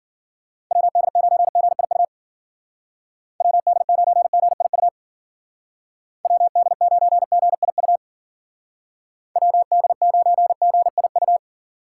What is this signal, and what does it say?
Morse code, heard as a keyed tone, about 49 wpm, 700 Hz. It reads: WD9GIU WD9GIU WD9GIU WD9GIU
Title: Speed Racing - USA 2x3 Call Signs 40wpm